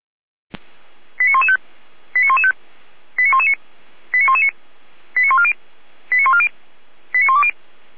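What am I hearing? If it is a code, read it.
Five-tone selective call (ZVEI), heard as quick runs of five tones, tones of about 70 ms. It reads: 89115 89115 89119 89119 89140 89140 89120
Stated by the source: ZVEI five-tone selective call, received on SDR.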